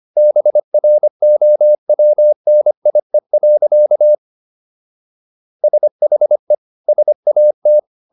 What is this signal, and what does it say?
Morse code, heard as a keyed tone, about 25 wpm, 600 Hz. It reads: BROWNIE. SHE SAT